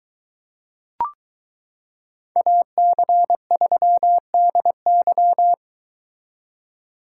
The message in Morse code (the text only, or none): AC3DY